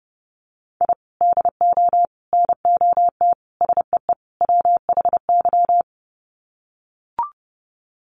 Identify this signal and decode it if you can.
Morse code, heard as a keyed tone, about 28 wpm, 700 Hz. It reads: I DO NOT SEE WHY
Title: Sets of 5 Words Encore 30wpm